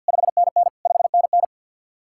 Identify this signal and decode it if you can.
Morse code, heard as a keyed tone, about 67 wpm, spaced as for 58 wpm, 700 Hz. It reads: 5NN 5NN
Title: Speed-Racing Common QSO Phrases Encore - 50wpm